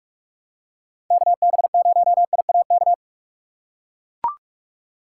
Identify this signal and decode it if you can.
Morse code, heard as a keyed tone, about 41 wpm, 700 Hz. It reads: KB0IAK
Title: USA Call Signs - 6x Character Spacing - 45wpm